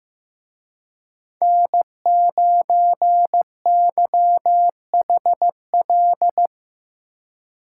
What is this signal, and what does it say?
Morse code, heard as a keyed tone, about 15 wpm, 700 Hz. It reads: N9YHL
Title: Speed Racing - USA Call Signs 15wpm